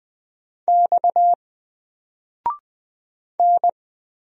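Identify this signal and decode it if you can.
Morse code, heard as a keyed tone, about 20 wpm, 700 Hz. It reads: X N
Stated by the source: Single letter and number Morse drill at 20 wpm, from N4LCD.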